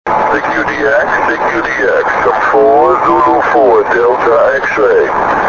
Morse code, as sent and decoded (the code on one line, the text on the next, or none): none